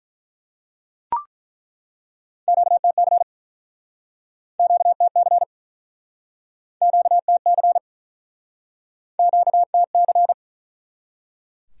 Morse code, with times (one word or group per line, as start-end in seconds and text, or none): 2.5-3.2 QTC
4.6-5.4 QTC
6.8-7.8 QTC
9.2-10.3 QTC